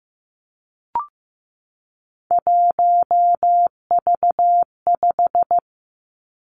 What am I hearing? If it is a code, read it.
Morse code, heard as a keyed tone, about 15 wpm, 700 Hz. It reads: 1V5
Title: Three Challenging Letter-Number Combinations 15wpm